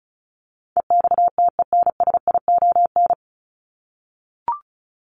Morse code, heard as a keyed tone, about 34 wpm, 700 Hz. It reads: EXTENSION
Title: Top 1501-1750 Words 35wpm